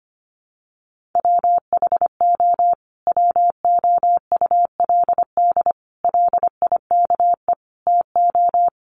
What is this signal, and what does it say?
Morse code, heard as a keyed tone, about 25 wpm, 700 Hz. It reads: WHO WOULD LIKE TO
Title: Sentences from Top 1500 Words 25wpm